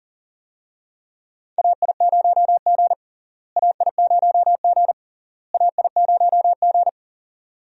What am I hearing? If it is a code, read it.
Morse code, heard as a keyed tone, about 37 wpm, 700 Hz. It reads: AI0G AI0G AI0G
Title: USA Call Signs - 4 Characters - Repeated 3x - 40wpm